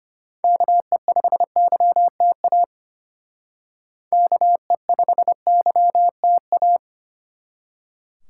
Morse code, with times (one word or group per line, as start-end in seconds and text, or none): 0.4-2.6 KE5YTA
4.1-6.8 KE5YTA